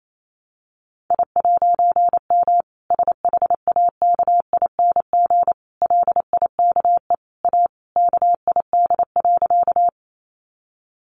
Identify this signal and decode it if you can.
Morse code, heard as a keyed tone, about 28 wpm, 700 Hz. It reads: I'M SHAKING LIKE A KID.
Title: A Sound Of Thunder 28wpm